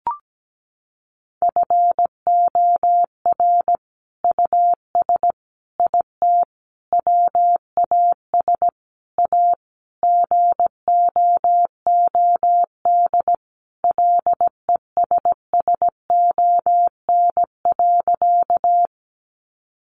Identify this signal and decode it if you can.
Morse code, heard as a keyed tone, about 17 wpm, 700 Hz. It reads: FOR US IT WAS A GOOD LESSON.
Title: Sentences from Top 1200 Words 17wpm